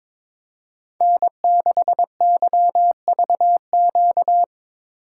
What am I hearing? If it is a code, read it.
Morse code, heard as a keyed tone, about 22 wpm, 700 Hz. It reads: N6YVQ
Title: USA Call Signs - 5 Characters - 8x Character Spacing - 22wpm